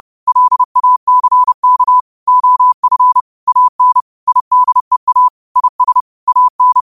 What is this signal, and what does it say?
Morse code, heard as a keyed tone, about 30 wpm, 1000 Hz. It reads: LACK OF AN IDEA IS AN
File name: qotdfiles / 30wpm